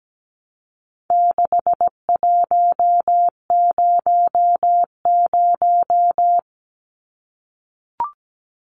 Morse code, heard as a keyed tone, about 17 wpm, 700 Hz. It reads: 6100
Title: Numbers - 4-Digit 17wpm